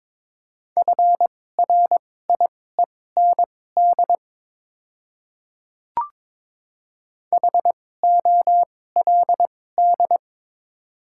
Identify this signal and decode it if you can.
Morse code, heard as a keyed tone, about 22 wpm, spaced as for 16 wpm, 700 Hz. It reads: FRIEND HOLD
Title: 2x Character Spacing - Top 500 Words - 22wpm